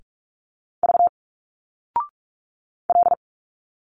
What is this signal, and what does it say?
Morse code, heard as a keyed tone, about 46 wpm, 700 Hz. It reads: V L
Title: Single Letters 45wpm